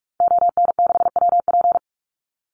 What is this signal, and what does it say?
Morse code, heard as a keyed tone, about 44 wpm, 700 Hz. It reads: ON6WP
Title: Call Signs - 45wpm